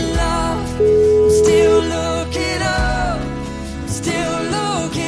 Ringback tone (Europe, 425 Hz), with noise.